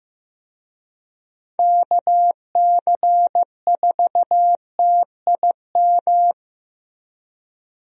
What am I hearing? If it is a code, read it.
Morse code, heard as a keyed tone, about 15 wpm, 700 Hz. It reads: KC4TIM
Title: USA Call Signs - 6 Characters - Repeated 3x - 15wpm